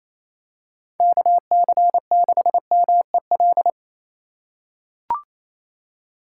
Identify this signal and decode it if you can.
Morse code, heard as a keyed tone, about 27 wpm, 700 Hz. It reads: KC6MEL
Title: USA Call Signs - 6 Characters - 2x Character Spacing - 28wpm